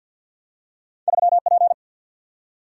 Morse code, heard as a keyed tone, about 52 wpm, 700 Hz. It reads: 3 P
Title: Single-Letter Single-Number 50wpm